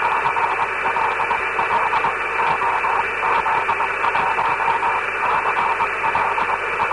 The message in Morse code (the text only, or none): none